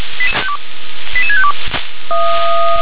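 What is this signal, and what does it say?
Five-tone selective call (ZVEI), heard as quick runs of five tones, tones of about 70 ms. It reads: 88652
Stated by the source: ZVEI five-tone selective call 88693 received on an SDR.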